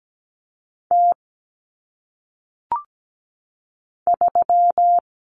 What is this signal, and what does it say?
Morse code, heard as a keyed tone, about 17 wpm, 700 Hz. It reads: T 3
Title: Common Cut Numbers - 1-Digit 17wpm